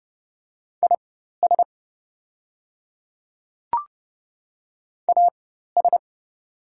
Morse code, heard as a keyed tone, about 30 wpm, spaced as for 14 wpm, 700 Hz. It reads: IS AS
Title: 4x Character Spacing - Top 200 Words - 30wpm